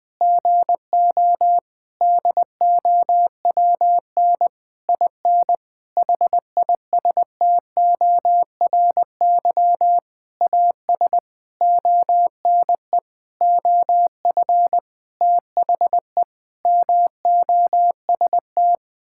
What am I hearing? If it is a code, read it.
Morse code, heard as a keyed tone, about 20 wpm, 700 Hz. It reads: GO DOWN IN HISTORY AS ONE OF THE MOST
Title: Sentences with 9-Word Phrases - Spoken Beginning - 20wpm